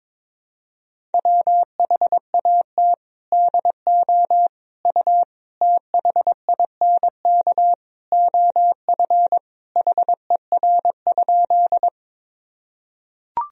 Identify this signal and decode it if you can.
Morse code, heard as a keyed tone, about 22 wpm, 700 Hz. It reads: WHAT DO U THINK OF HER?
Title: Sentences from Top 200 Words and Abbreviations 22wpm